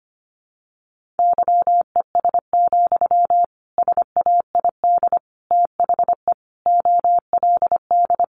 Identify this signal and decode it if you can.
Morse code, heard as a keyed tone, about 25 wpm, 700 Hz. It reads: YES, SAID THE OLD
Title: Sentences from Top 500 Words 25wpm